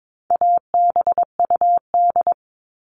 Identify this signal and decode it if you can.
Morse code, heard as a keyed tone, about 22 wpm, 700 Hz. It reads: ABUD
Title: All Cut Numbers - 4-Digit 22wpm